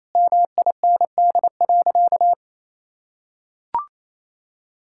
Morse code, heard as a keyed tone, about 28 wpm, 700 Hz. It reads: MIND.
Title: Sentences from Top 1000 Words - Review Part 5 - 28wpm